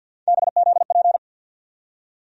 Morse code, heard as a keyed tone, about 47 wpm, 700 Hz. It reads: BZP
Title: USA Call Sign Postfixes - 3 Letters 50wpm